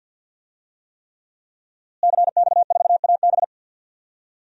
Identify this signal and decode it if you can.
Morse code, heard as a keyed tone, about 47 wpm, 700 Hz. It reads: CX4AB
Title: DXCC-100 Call Signs - 5 Characters - Repeated 3x - 50wpm